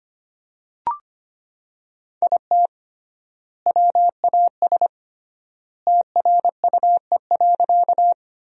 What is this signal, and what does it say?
Morse code, heard as a keyed tone, about 25 wpm, 700 Hz. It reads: IT WAS TRUE.
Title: 3x Word-Spacing - Sentences from Top 400 Words - 25wpm